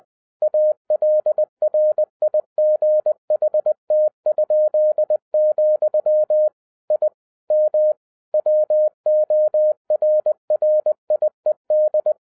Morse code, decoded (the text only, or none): ALRIGHT?, I M WORRIED